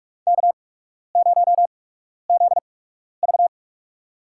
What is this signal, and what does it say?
Morse code, heard as a keyed tone, about 43 wpm, spaced as for 12 wpm, 700 Hz. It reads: K0ZV
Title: USA Call Signs - 4 Characters - 8x Character Spacing - 45wpm